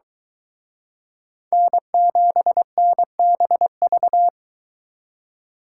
Morse code, heard as a keyed tone, about 23 wpm, 700 Hz. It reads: N7NBV